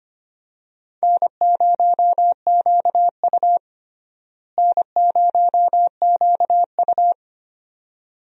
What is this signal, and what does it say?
Morse code, heard as a keyed tone, about 25 wpm, 700 Hz. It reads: N0QU N0QU
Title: USA Call Signs - 4 Characters - Repeated 2x - 25wpm